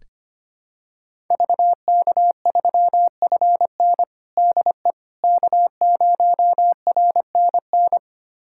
Morse code, heard as a keyed tone, about 25 wpm, 700 Hz. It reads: VK3FN DE K0RNN
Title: USA Callsign DE Callsign - 5 Characters - 25wpm